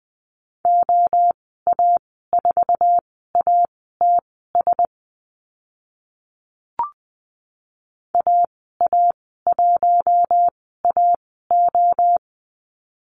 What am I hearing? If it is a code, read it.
Morse code, heard as a keyed tone, about 20 wpm, spaced as for 14 wpm, 700 Hz. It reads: OA4ATS AA1AO
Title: 069 Lesson 12g - Letter d - Call Signs 20wpm